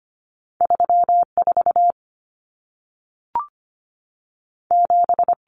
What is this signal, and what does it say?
Morse code, heard as a keyed tone, about 25 wpm, 700 Hz. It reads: 3 4 7
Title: Common Cut Numbers - 2-Digit 25wpm